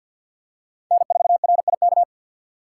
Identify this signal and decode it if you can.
Morse code, heard as a keyed tone, about 46 wpm, 700 Hz. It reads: N4RIK
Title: USA Call Signs - 6x Character Spacing - 50wpm